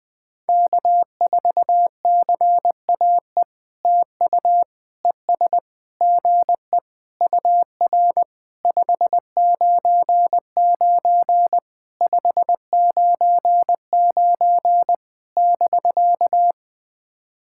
Morse code, heard as a keyed tone, about 21 wpm, 700 Hz. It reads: K4CAE TU ES GE UR 599 599 <BK>
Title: POTA QSO Practice - 20wpm